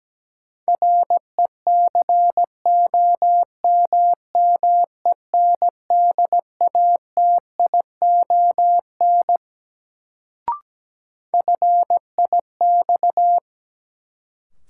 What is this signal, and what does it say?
Morse code, heard as a keyed tone, about 17 wpm, 700 Hz. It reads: RECOMMENDATION FIX
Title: Sentences from Top 1400 Words - New Word Drill Alt - 17wpm